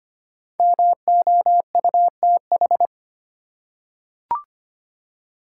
Morse code, heard as a keyed tone, about 25 wpm, 700 Hz. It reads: MOUTH
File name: 5 Letter Words - First 3 Letters Given - 25wpm